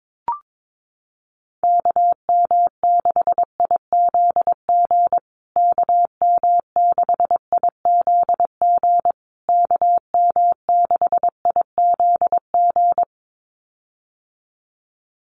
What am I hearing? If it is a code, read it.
Morse code, heard as a keyed tone, about 22 wpm, 700 Hz. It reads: KM6IZG KM6IZG KM6IZG
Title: USA Call Signs Repeated 3x 22wpm